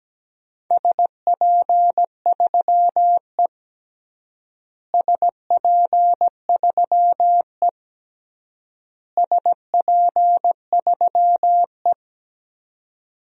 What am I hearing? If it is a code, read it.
Morse code, heard as a keyed tone, about 17 wpm, 700 Hz. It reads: SP3E SP3E SP3E
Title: DXCC-100 Call Signs - 4 Characters - Repeated 3x - 17wpm